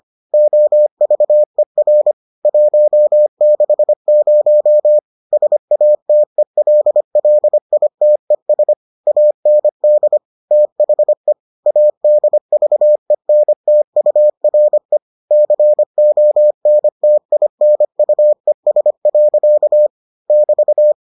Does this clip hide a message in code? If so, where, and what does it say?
Yes: Morse code from 0.3 to 20.9 s, heard as a keyed tone, about 25 wpm, 600 Hz. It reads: OVER 160 SATELLITES AND THE ADVENTURE CONTINUES. <BT>